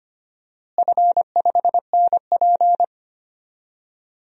Morse code, heard as a keyed tone, about 25 wpm, 700 Hz. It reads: F5NP
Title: DXCC-100 Call Signs - 4 Characters - Repeated 3x - 25wpm